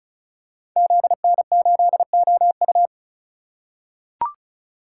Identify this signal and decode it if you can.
Morse code, heard as a keyed tone, about 36 wpm, 690 Hz. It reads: ZN8OU
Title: Code Groups - 5 Letter:Numbers - 35wpm